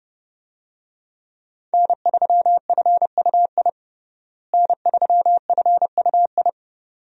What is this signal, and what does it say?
Morse code, heard as a keyed tone, about 30 wpm, 700 Hz. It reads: N3FUI N3FUI
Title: USA Call Signs - 5 Characters - Repeated 2x - 30wpm